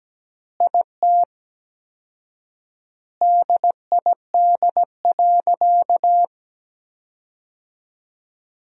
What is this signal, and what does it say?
Morse code, heard as a keyed tone, about 17 wpm, 700 Hz. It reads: IT DID.
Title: 4x Word-Spacing - Sentences from Top 500 Words - 17wpm